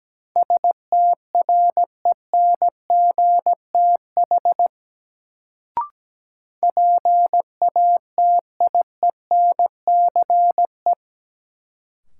Morse code, heard as a keyed tone, about 17 wpm, 700 Hz. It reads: STRENGTH PATIENCE
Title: Sentences from Top 1300 Words - New Word Drill Alt - 17wpm